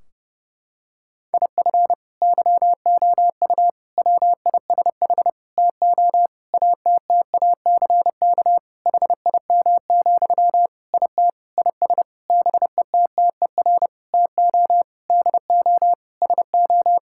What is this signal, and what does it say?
Morse code, heard as a keyed tone, about 30 wpm, 700 Hz. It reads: IF YOU WISH TO ATTACK HIM, IT IS BETTER TO DO SO
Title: Sentences from Top 1600 Words 30wpm